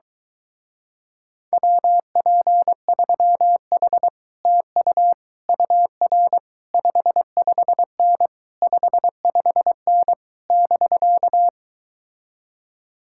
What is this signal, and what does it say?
Morse code, heard as a keyed tone, about 23 wpm, 700 Hz. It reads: WP3H TU UR 55N 55N <BK>